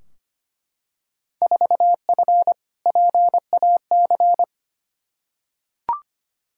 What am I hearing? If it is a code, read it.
Morse code, heard as a keyed tone, about 25 wpm, 700 Hz. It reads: 4F PAC
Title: Field Day Exchanges Encore - 25wpm